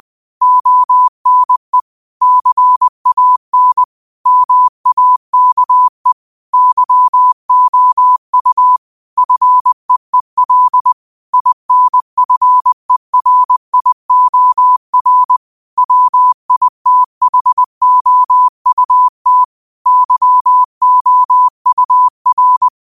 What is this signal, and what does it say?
Morse code, heard as a keyed tone, about 20 wpm, 1000 Hz. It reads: ONE CAN MAKE YOU FEEL INFERIOR WITHOUT YOUR
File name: qotdfiles / 20wpm